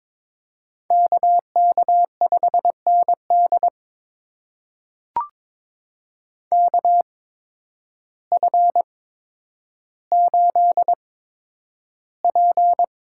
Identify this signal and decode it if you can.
Morse code, heard as a keyed tone, about 21 wpm, spaced as for 10 wpm, 700 Hz. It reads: KK5ND KF8P
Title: USA Call Signs - 5 Characters - 8x Character Spacing - 22wpm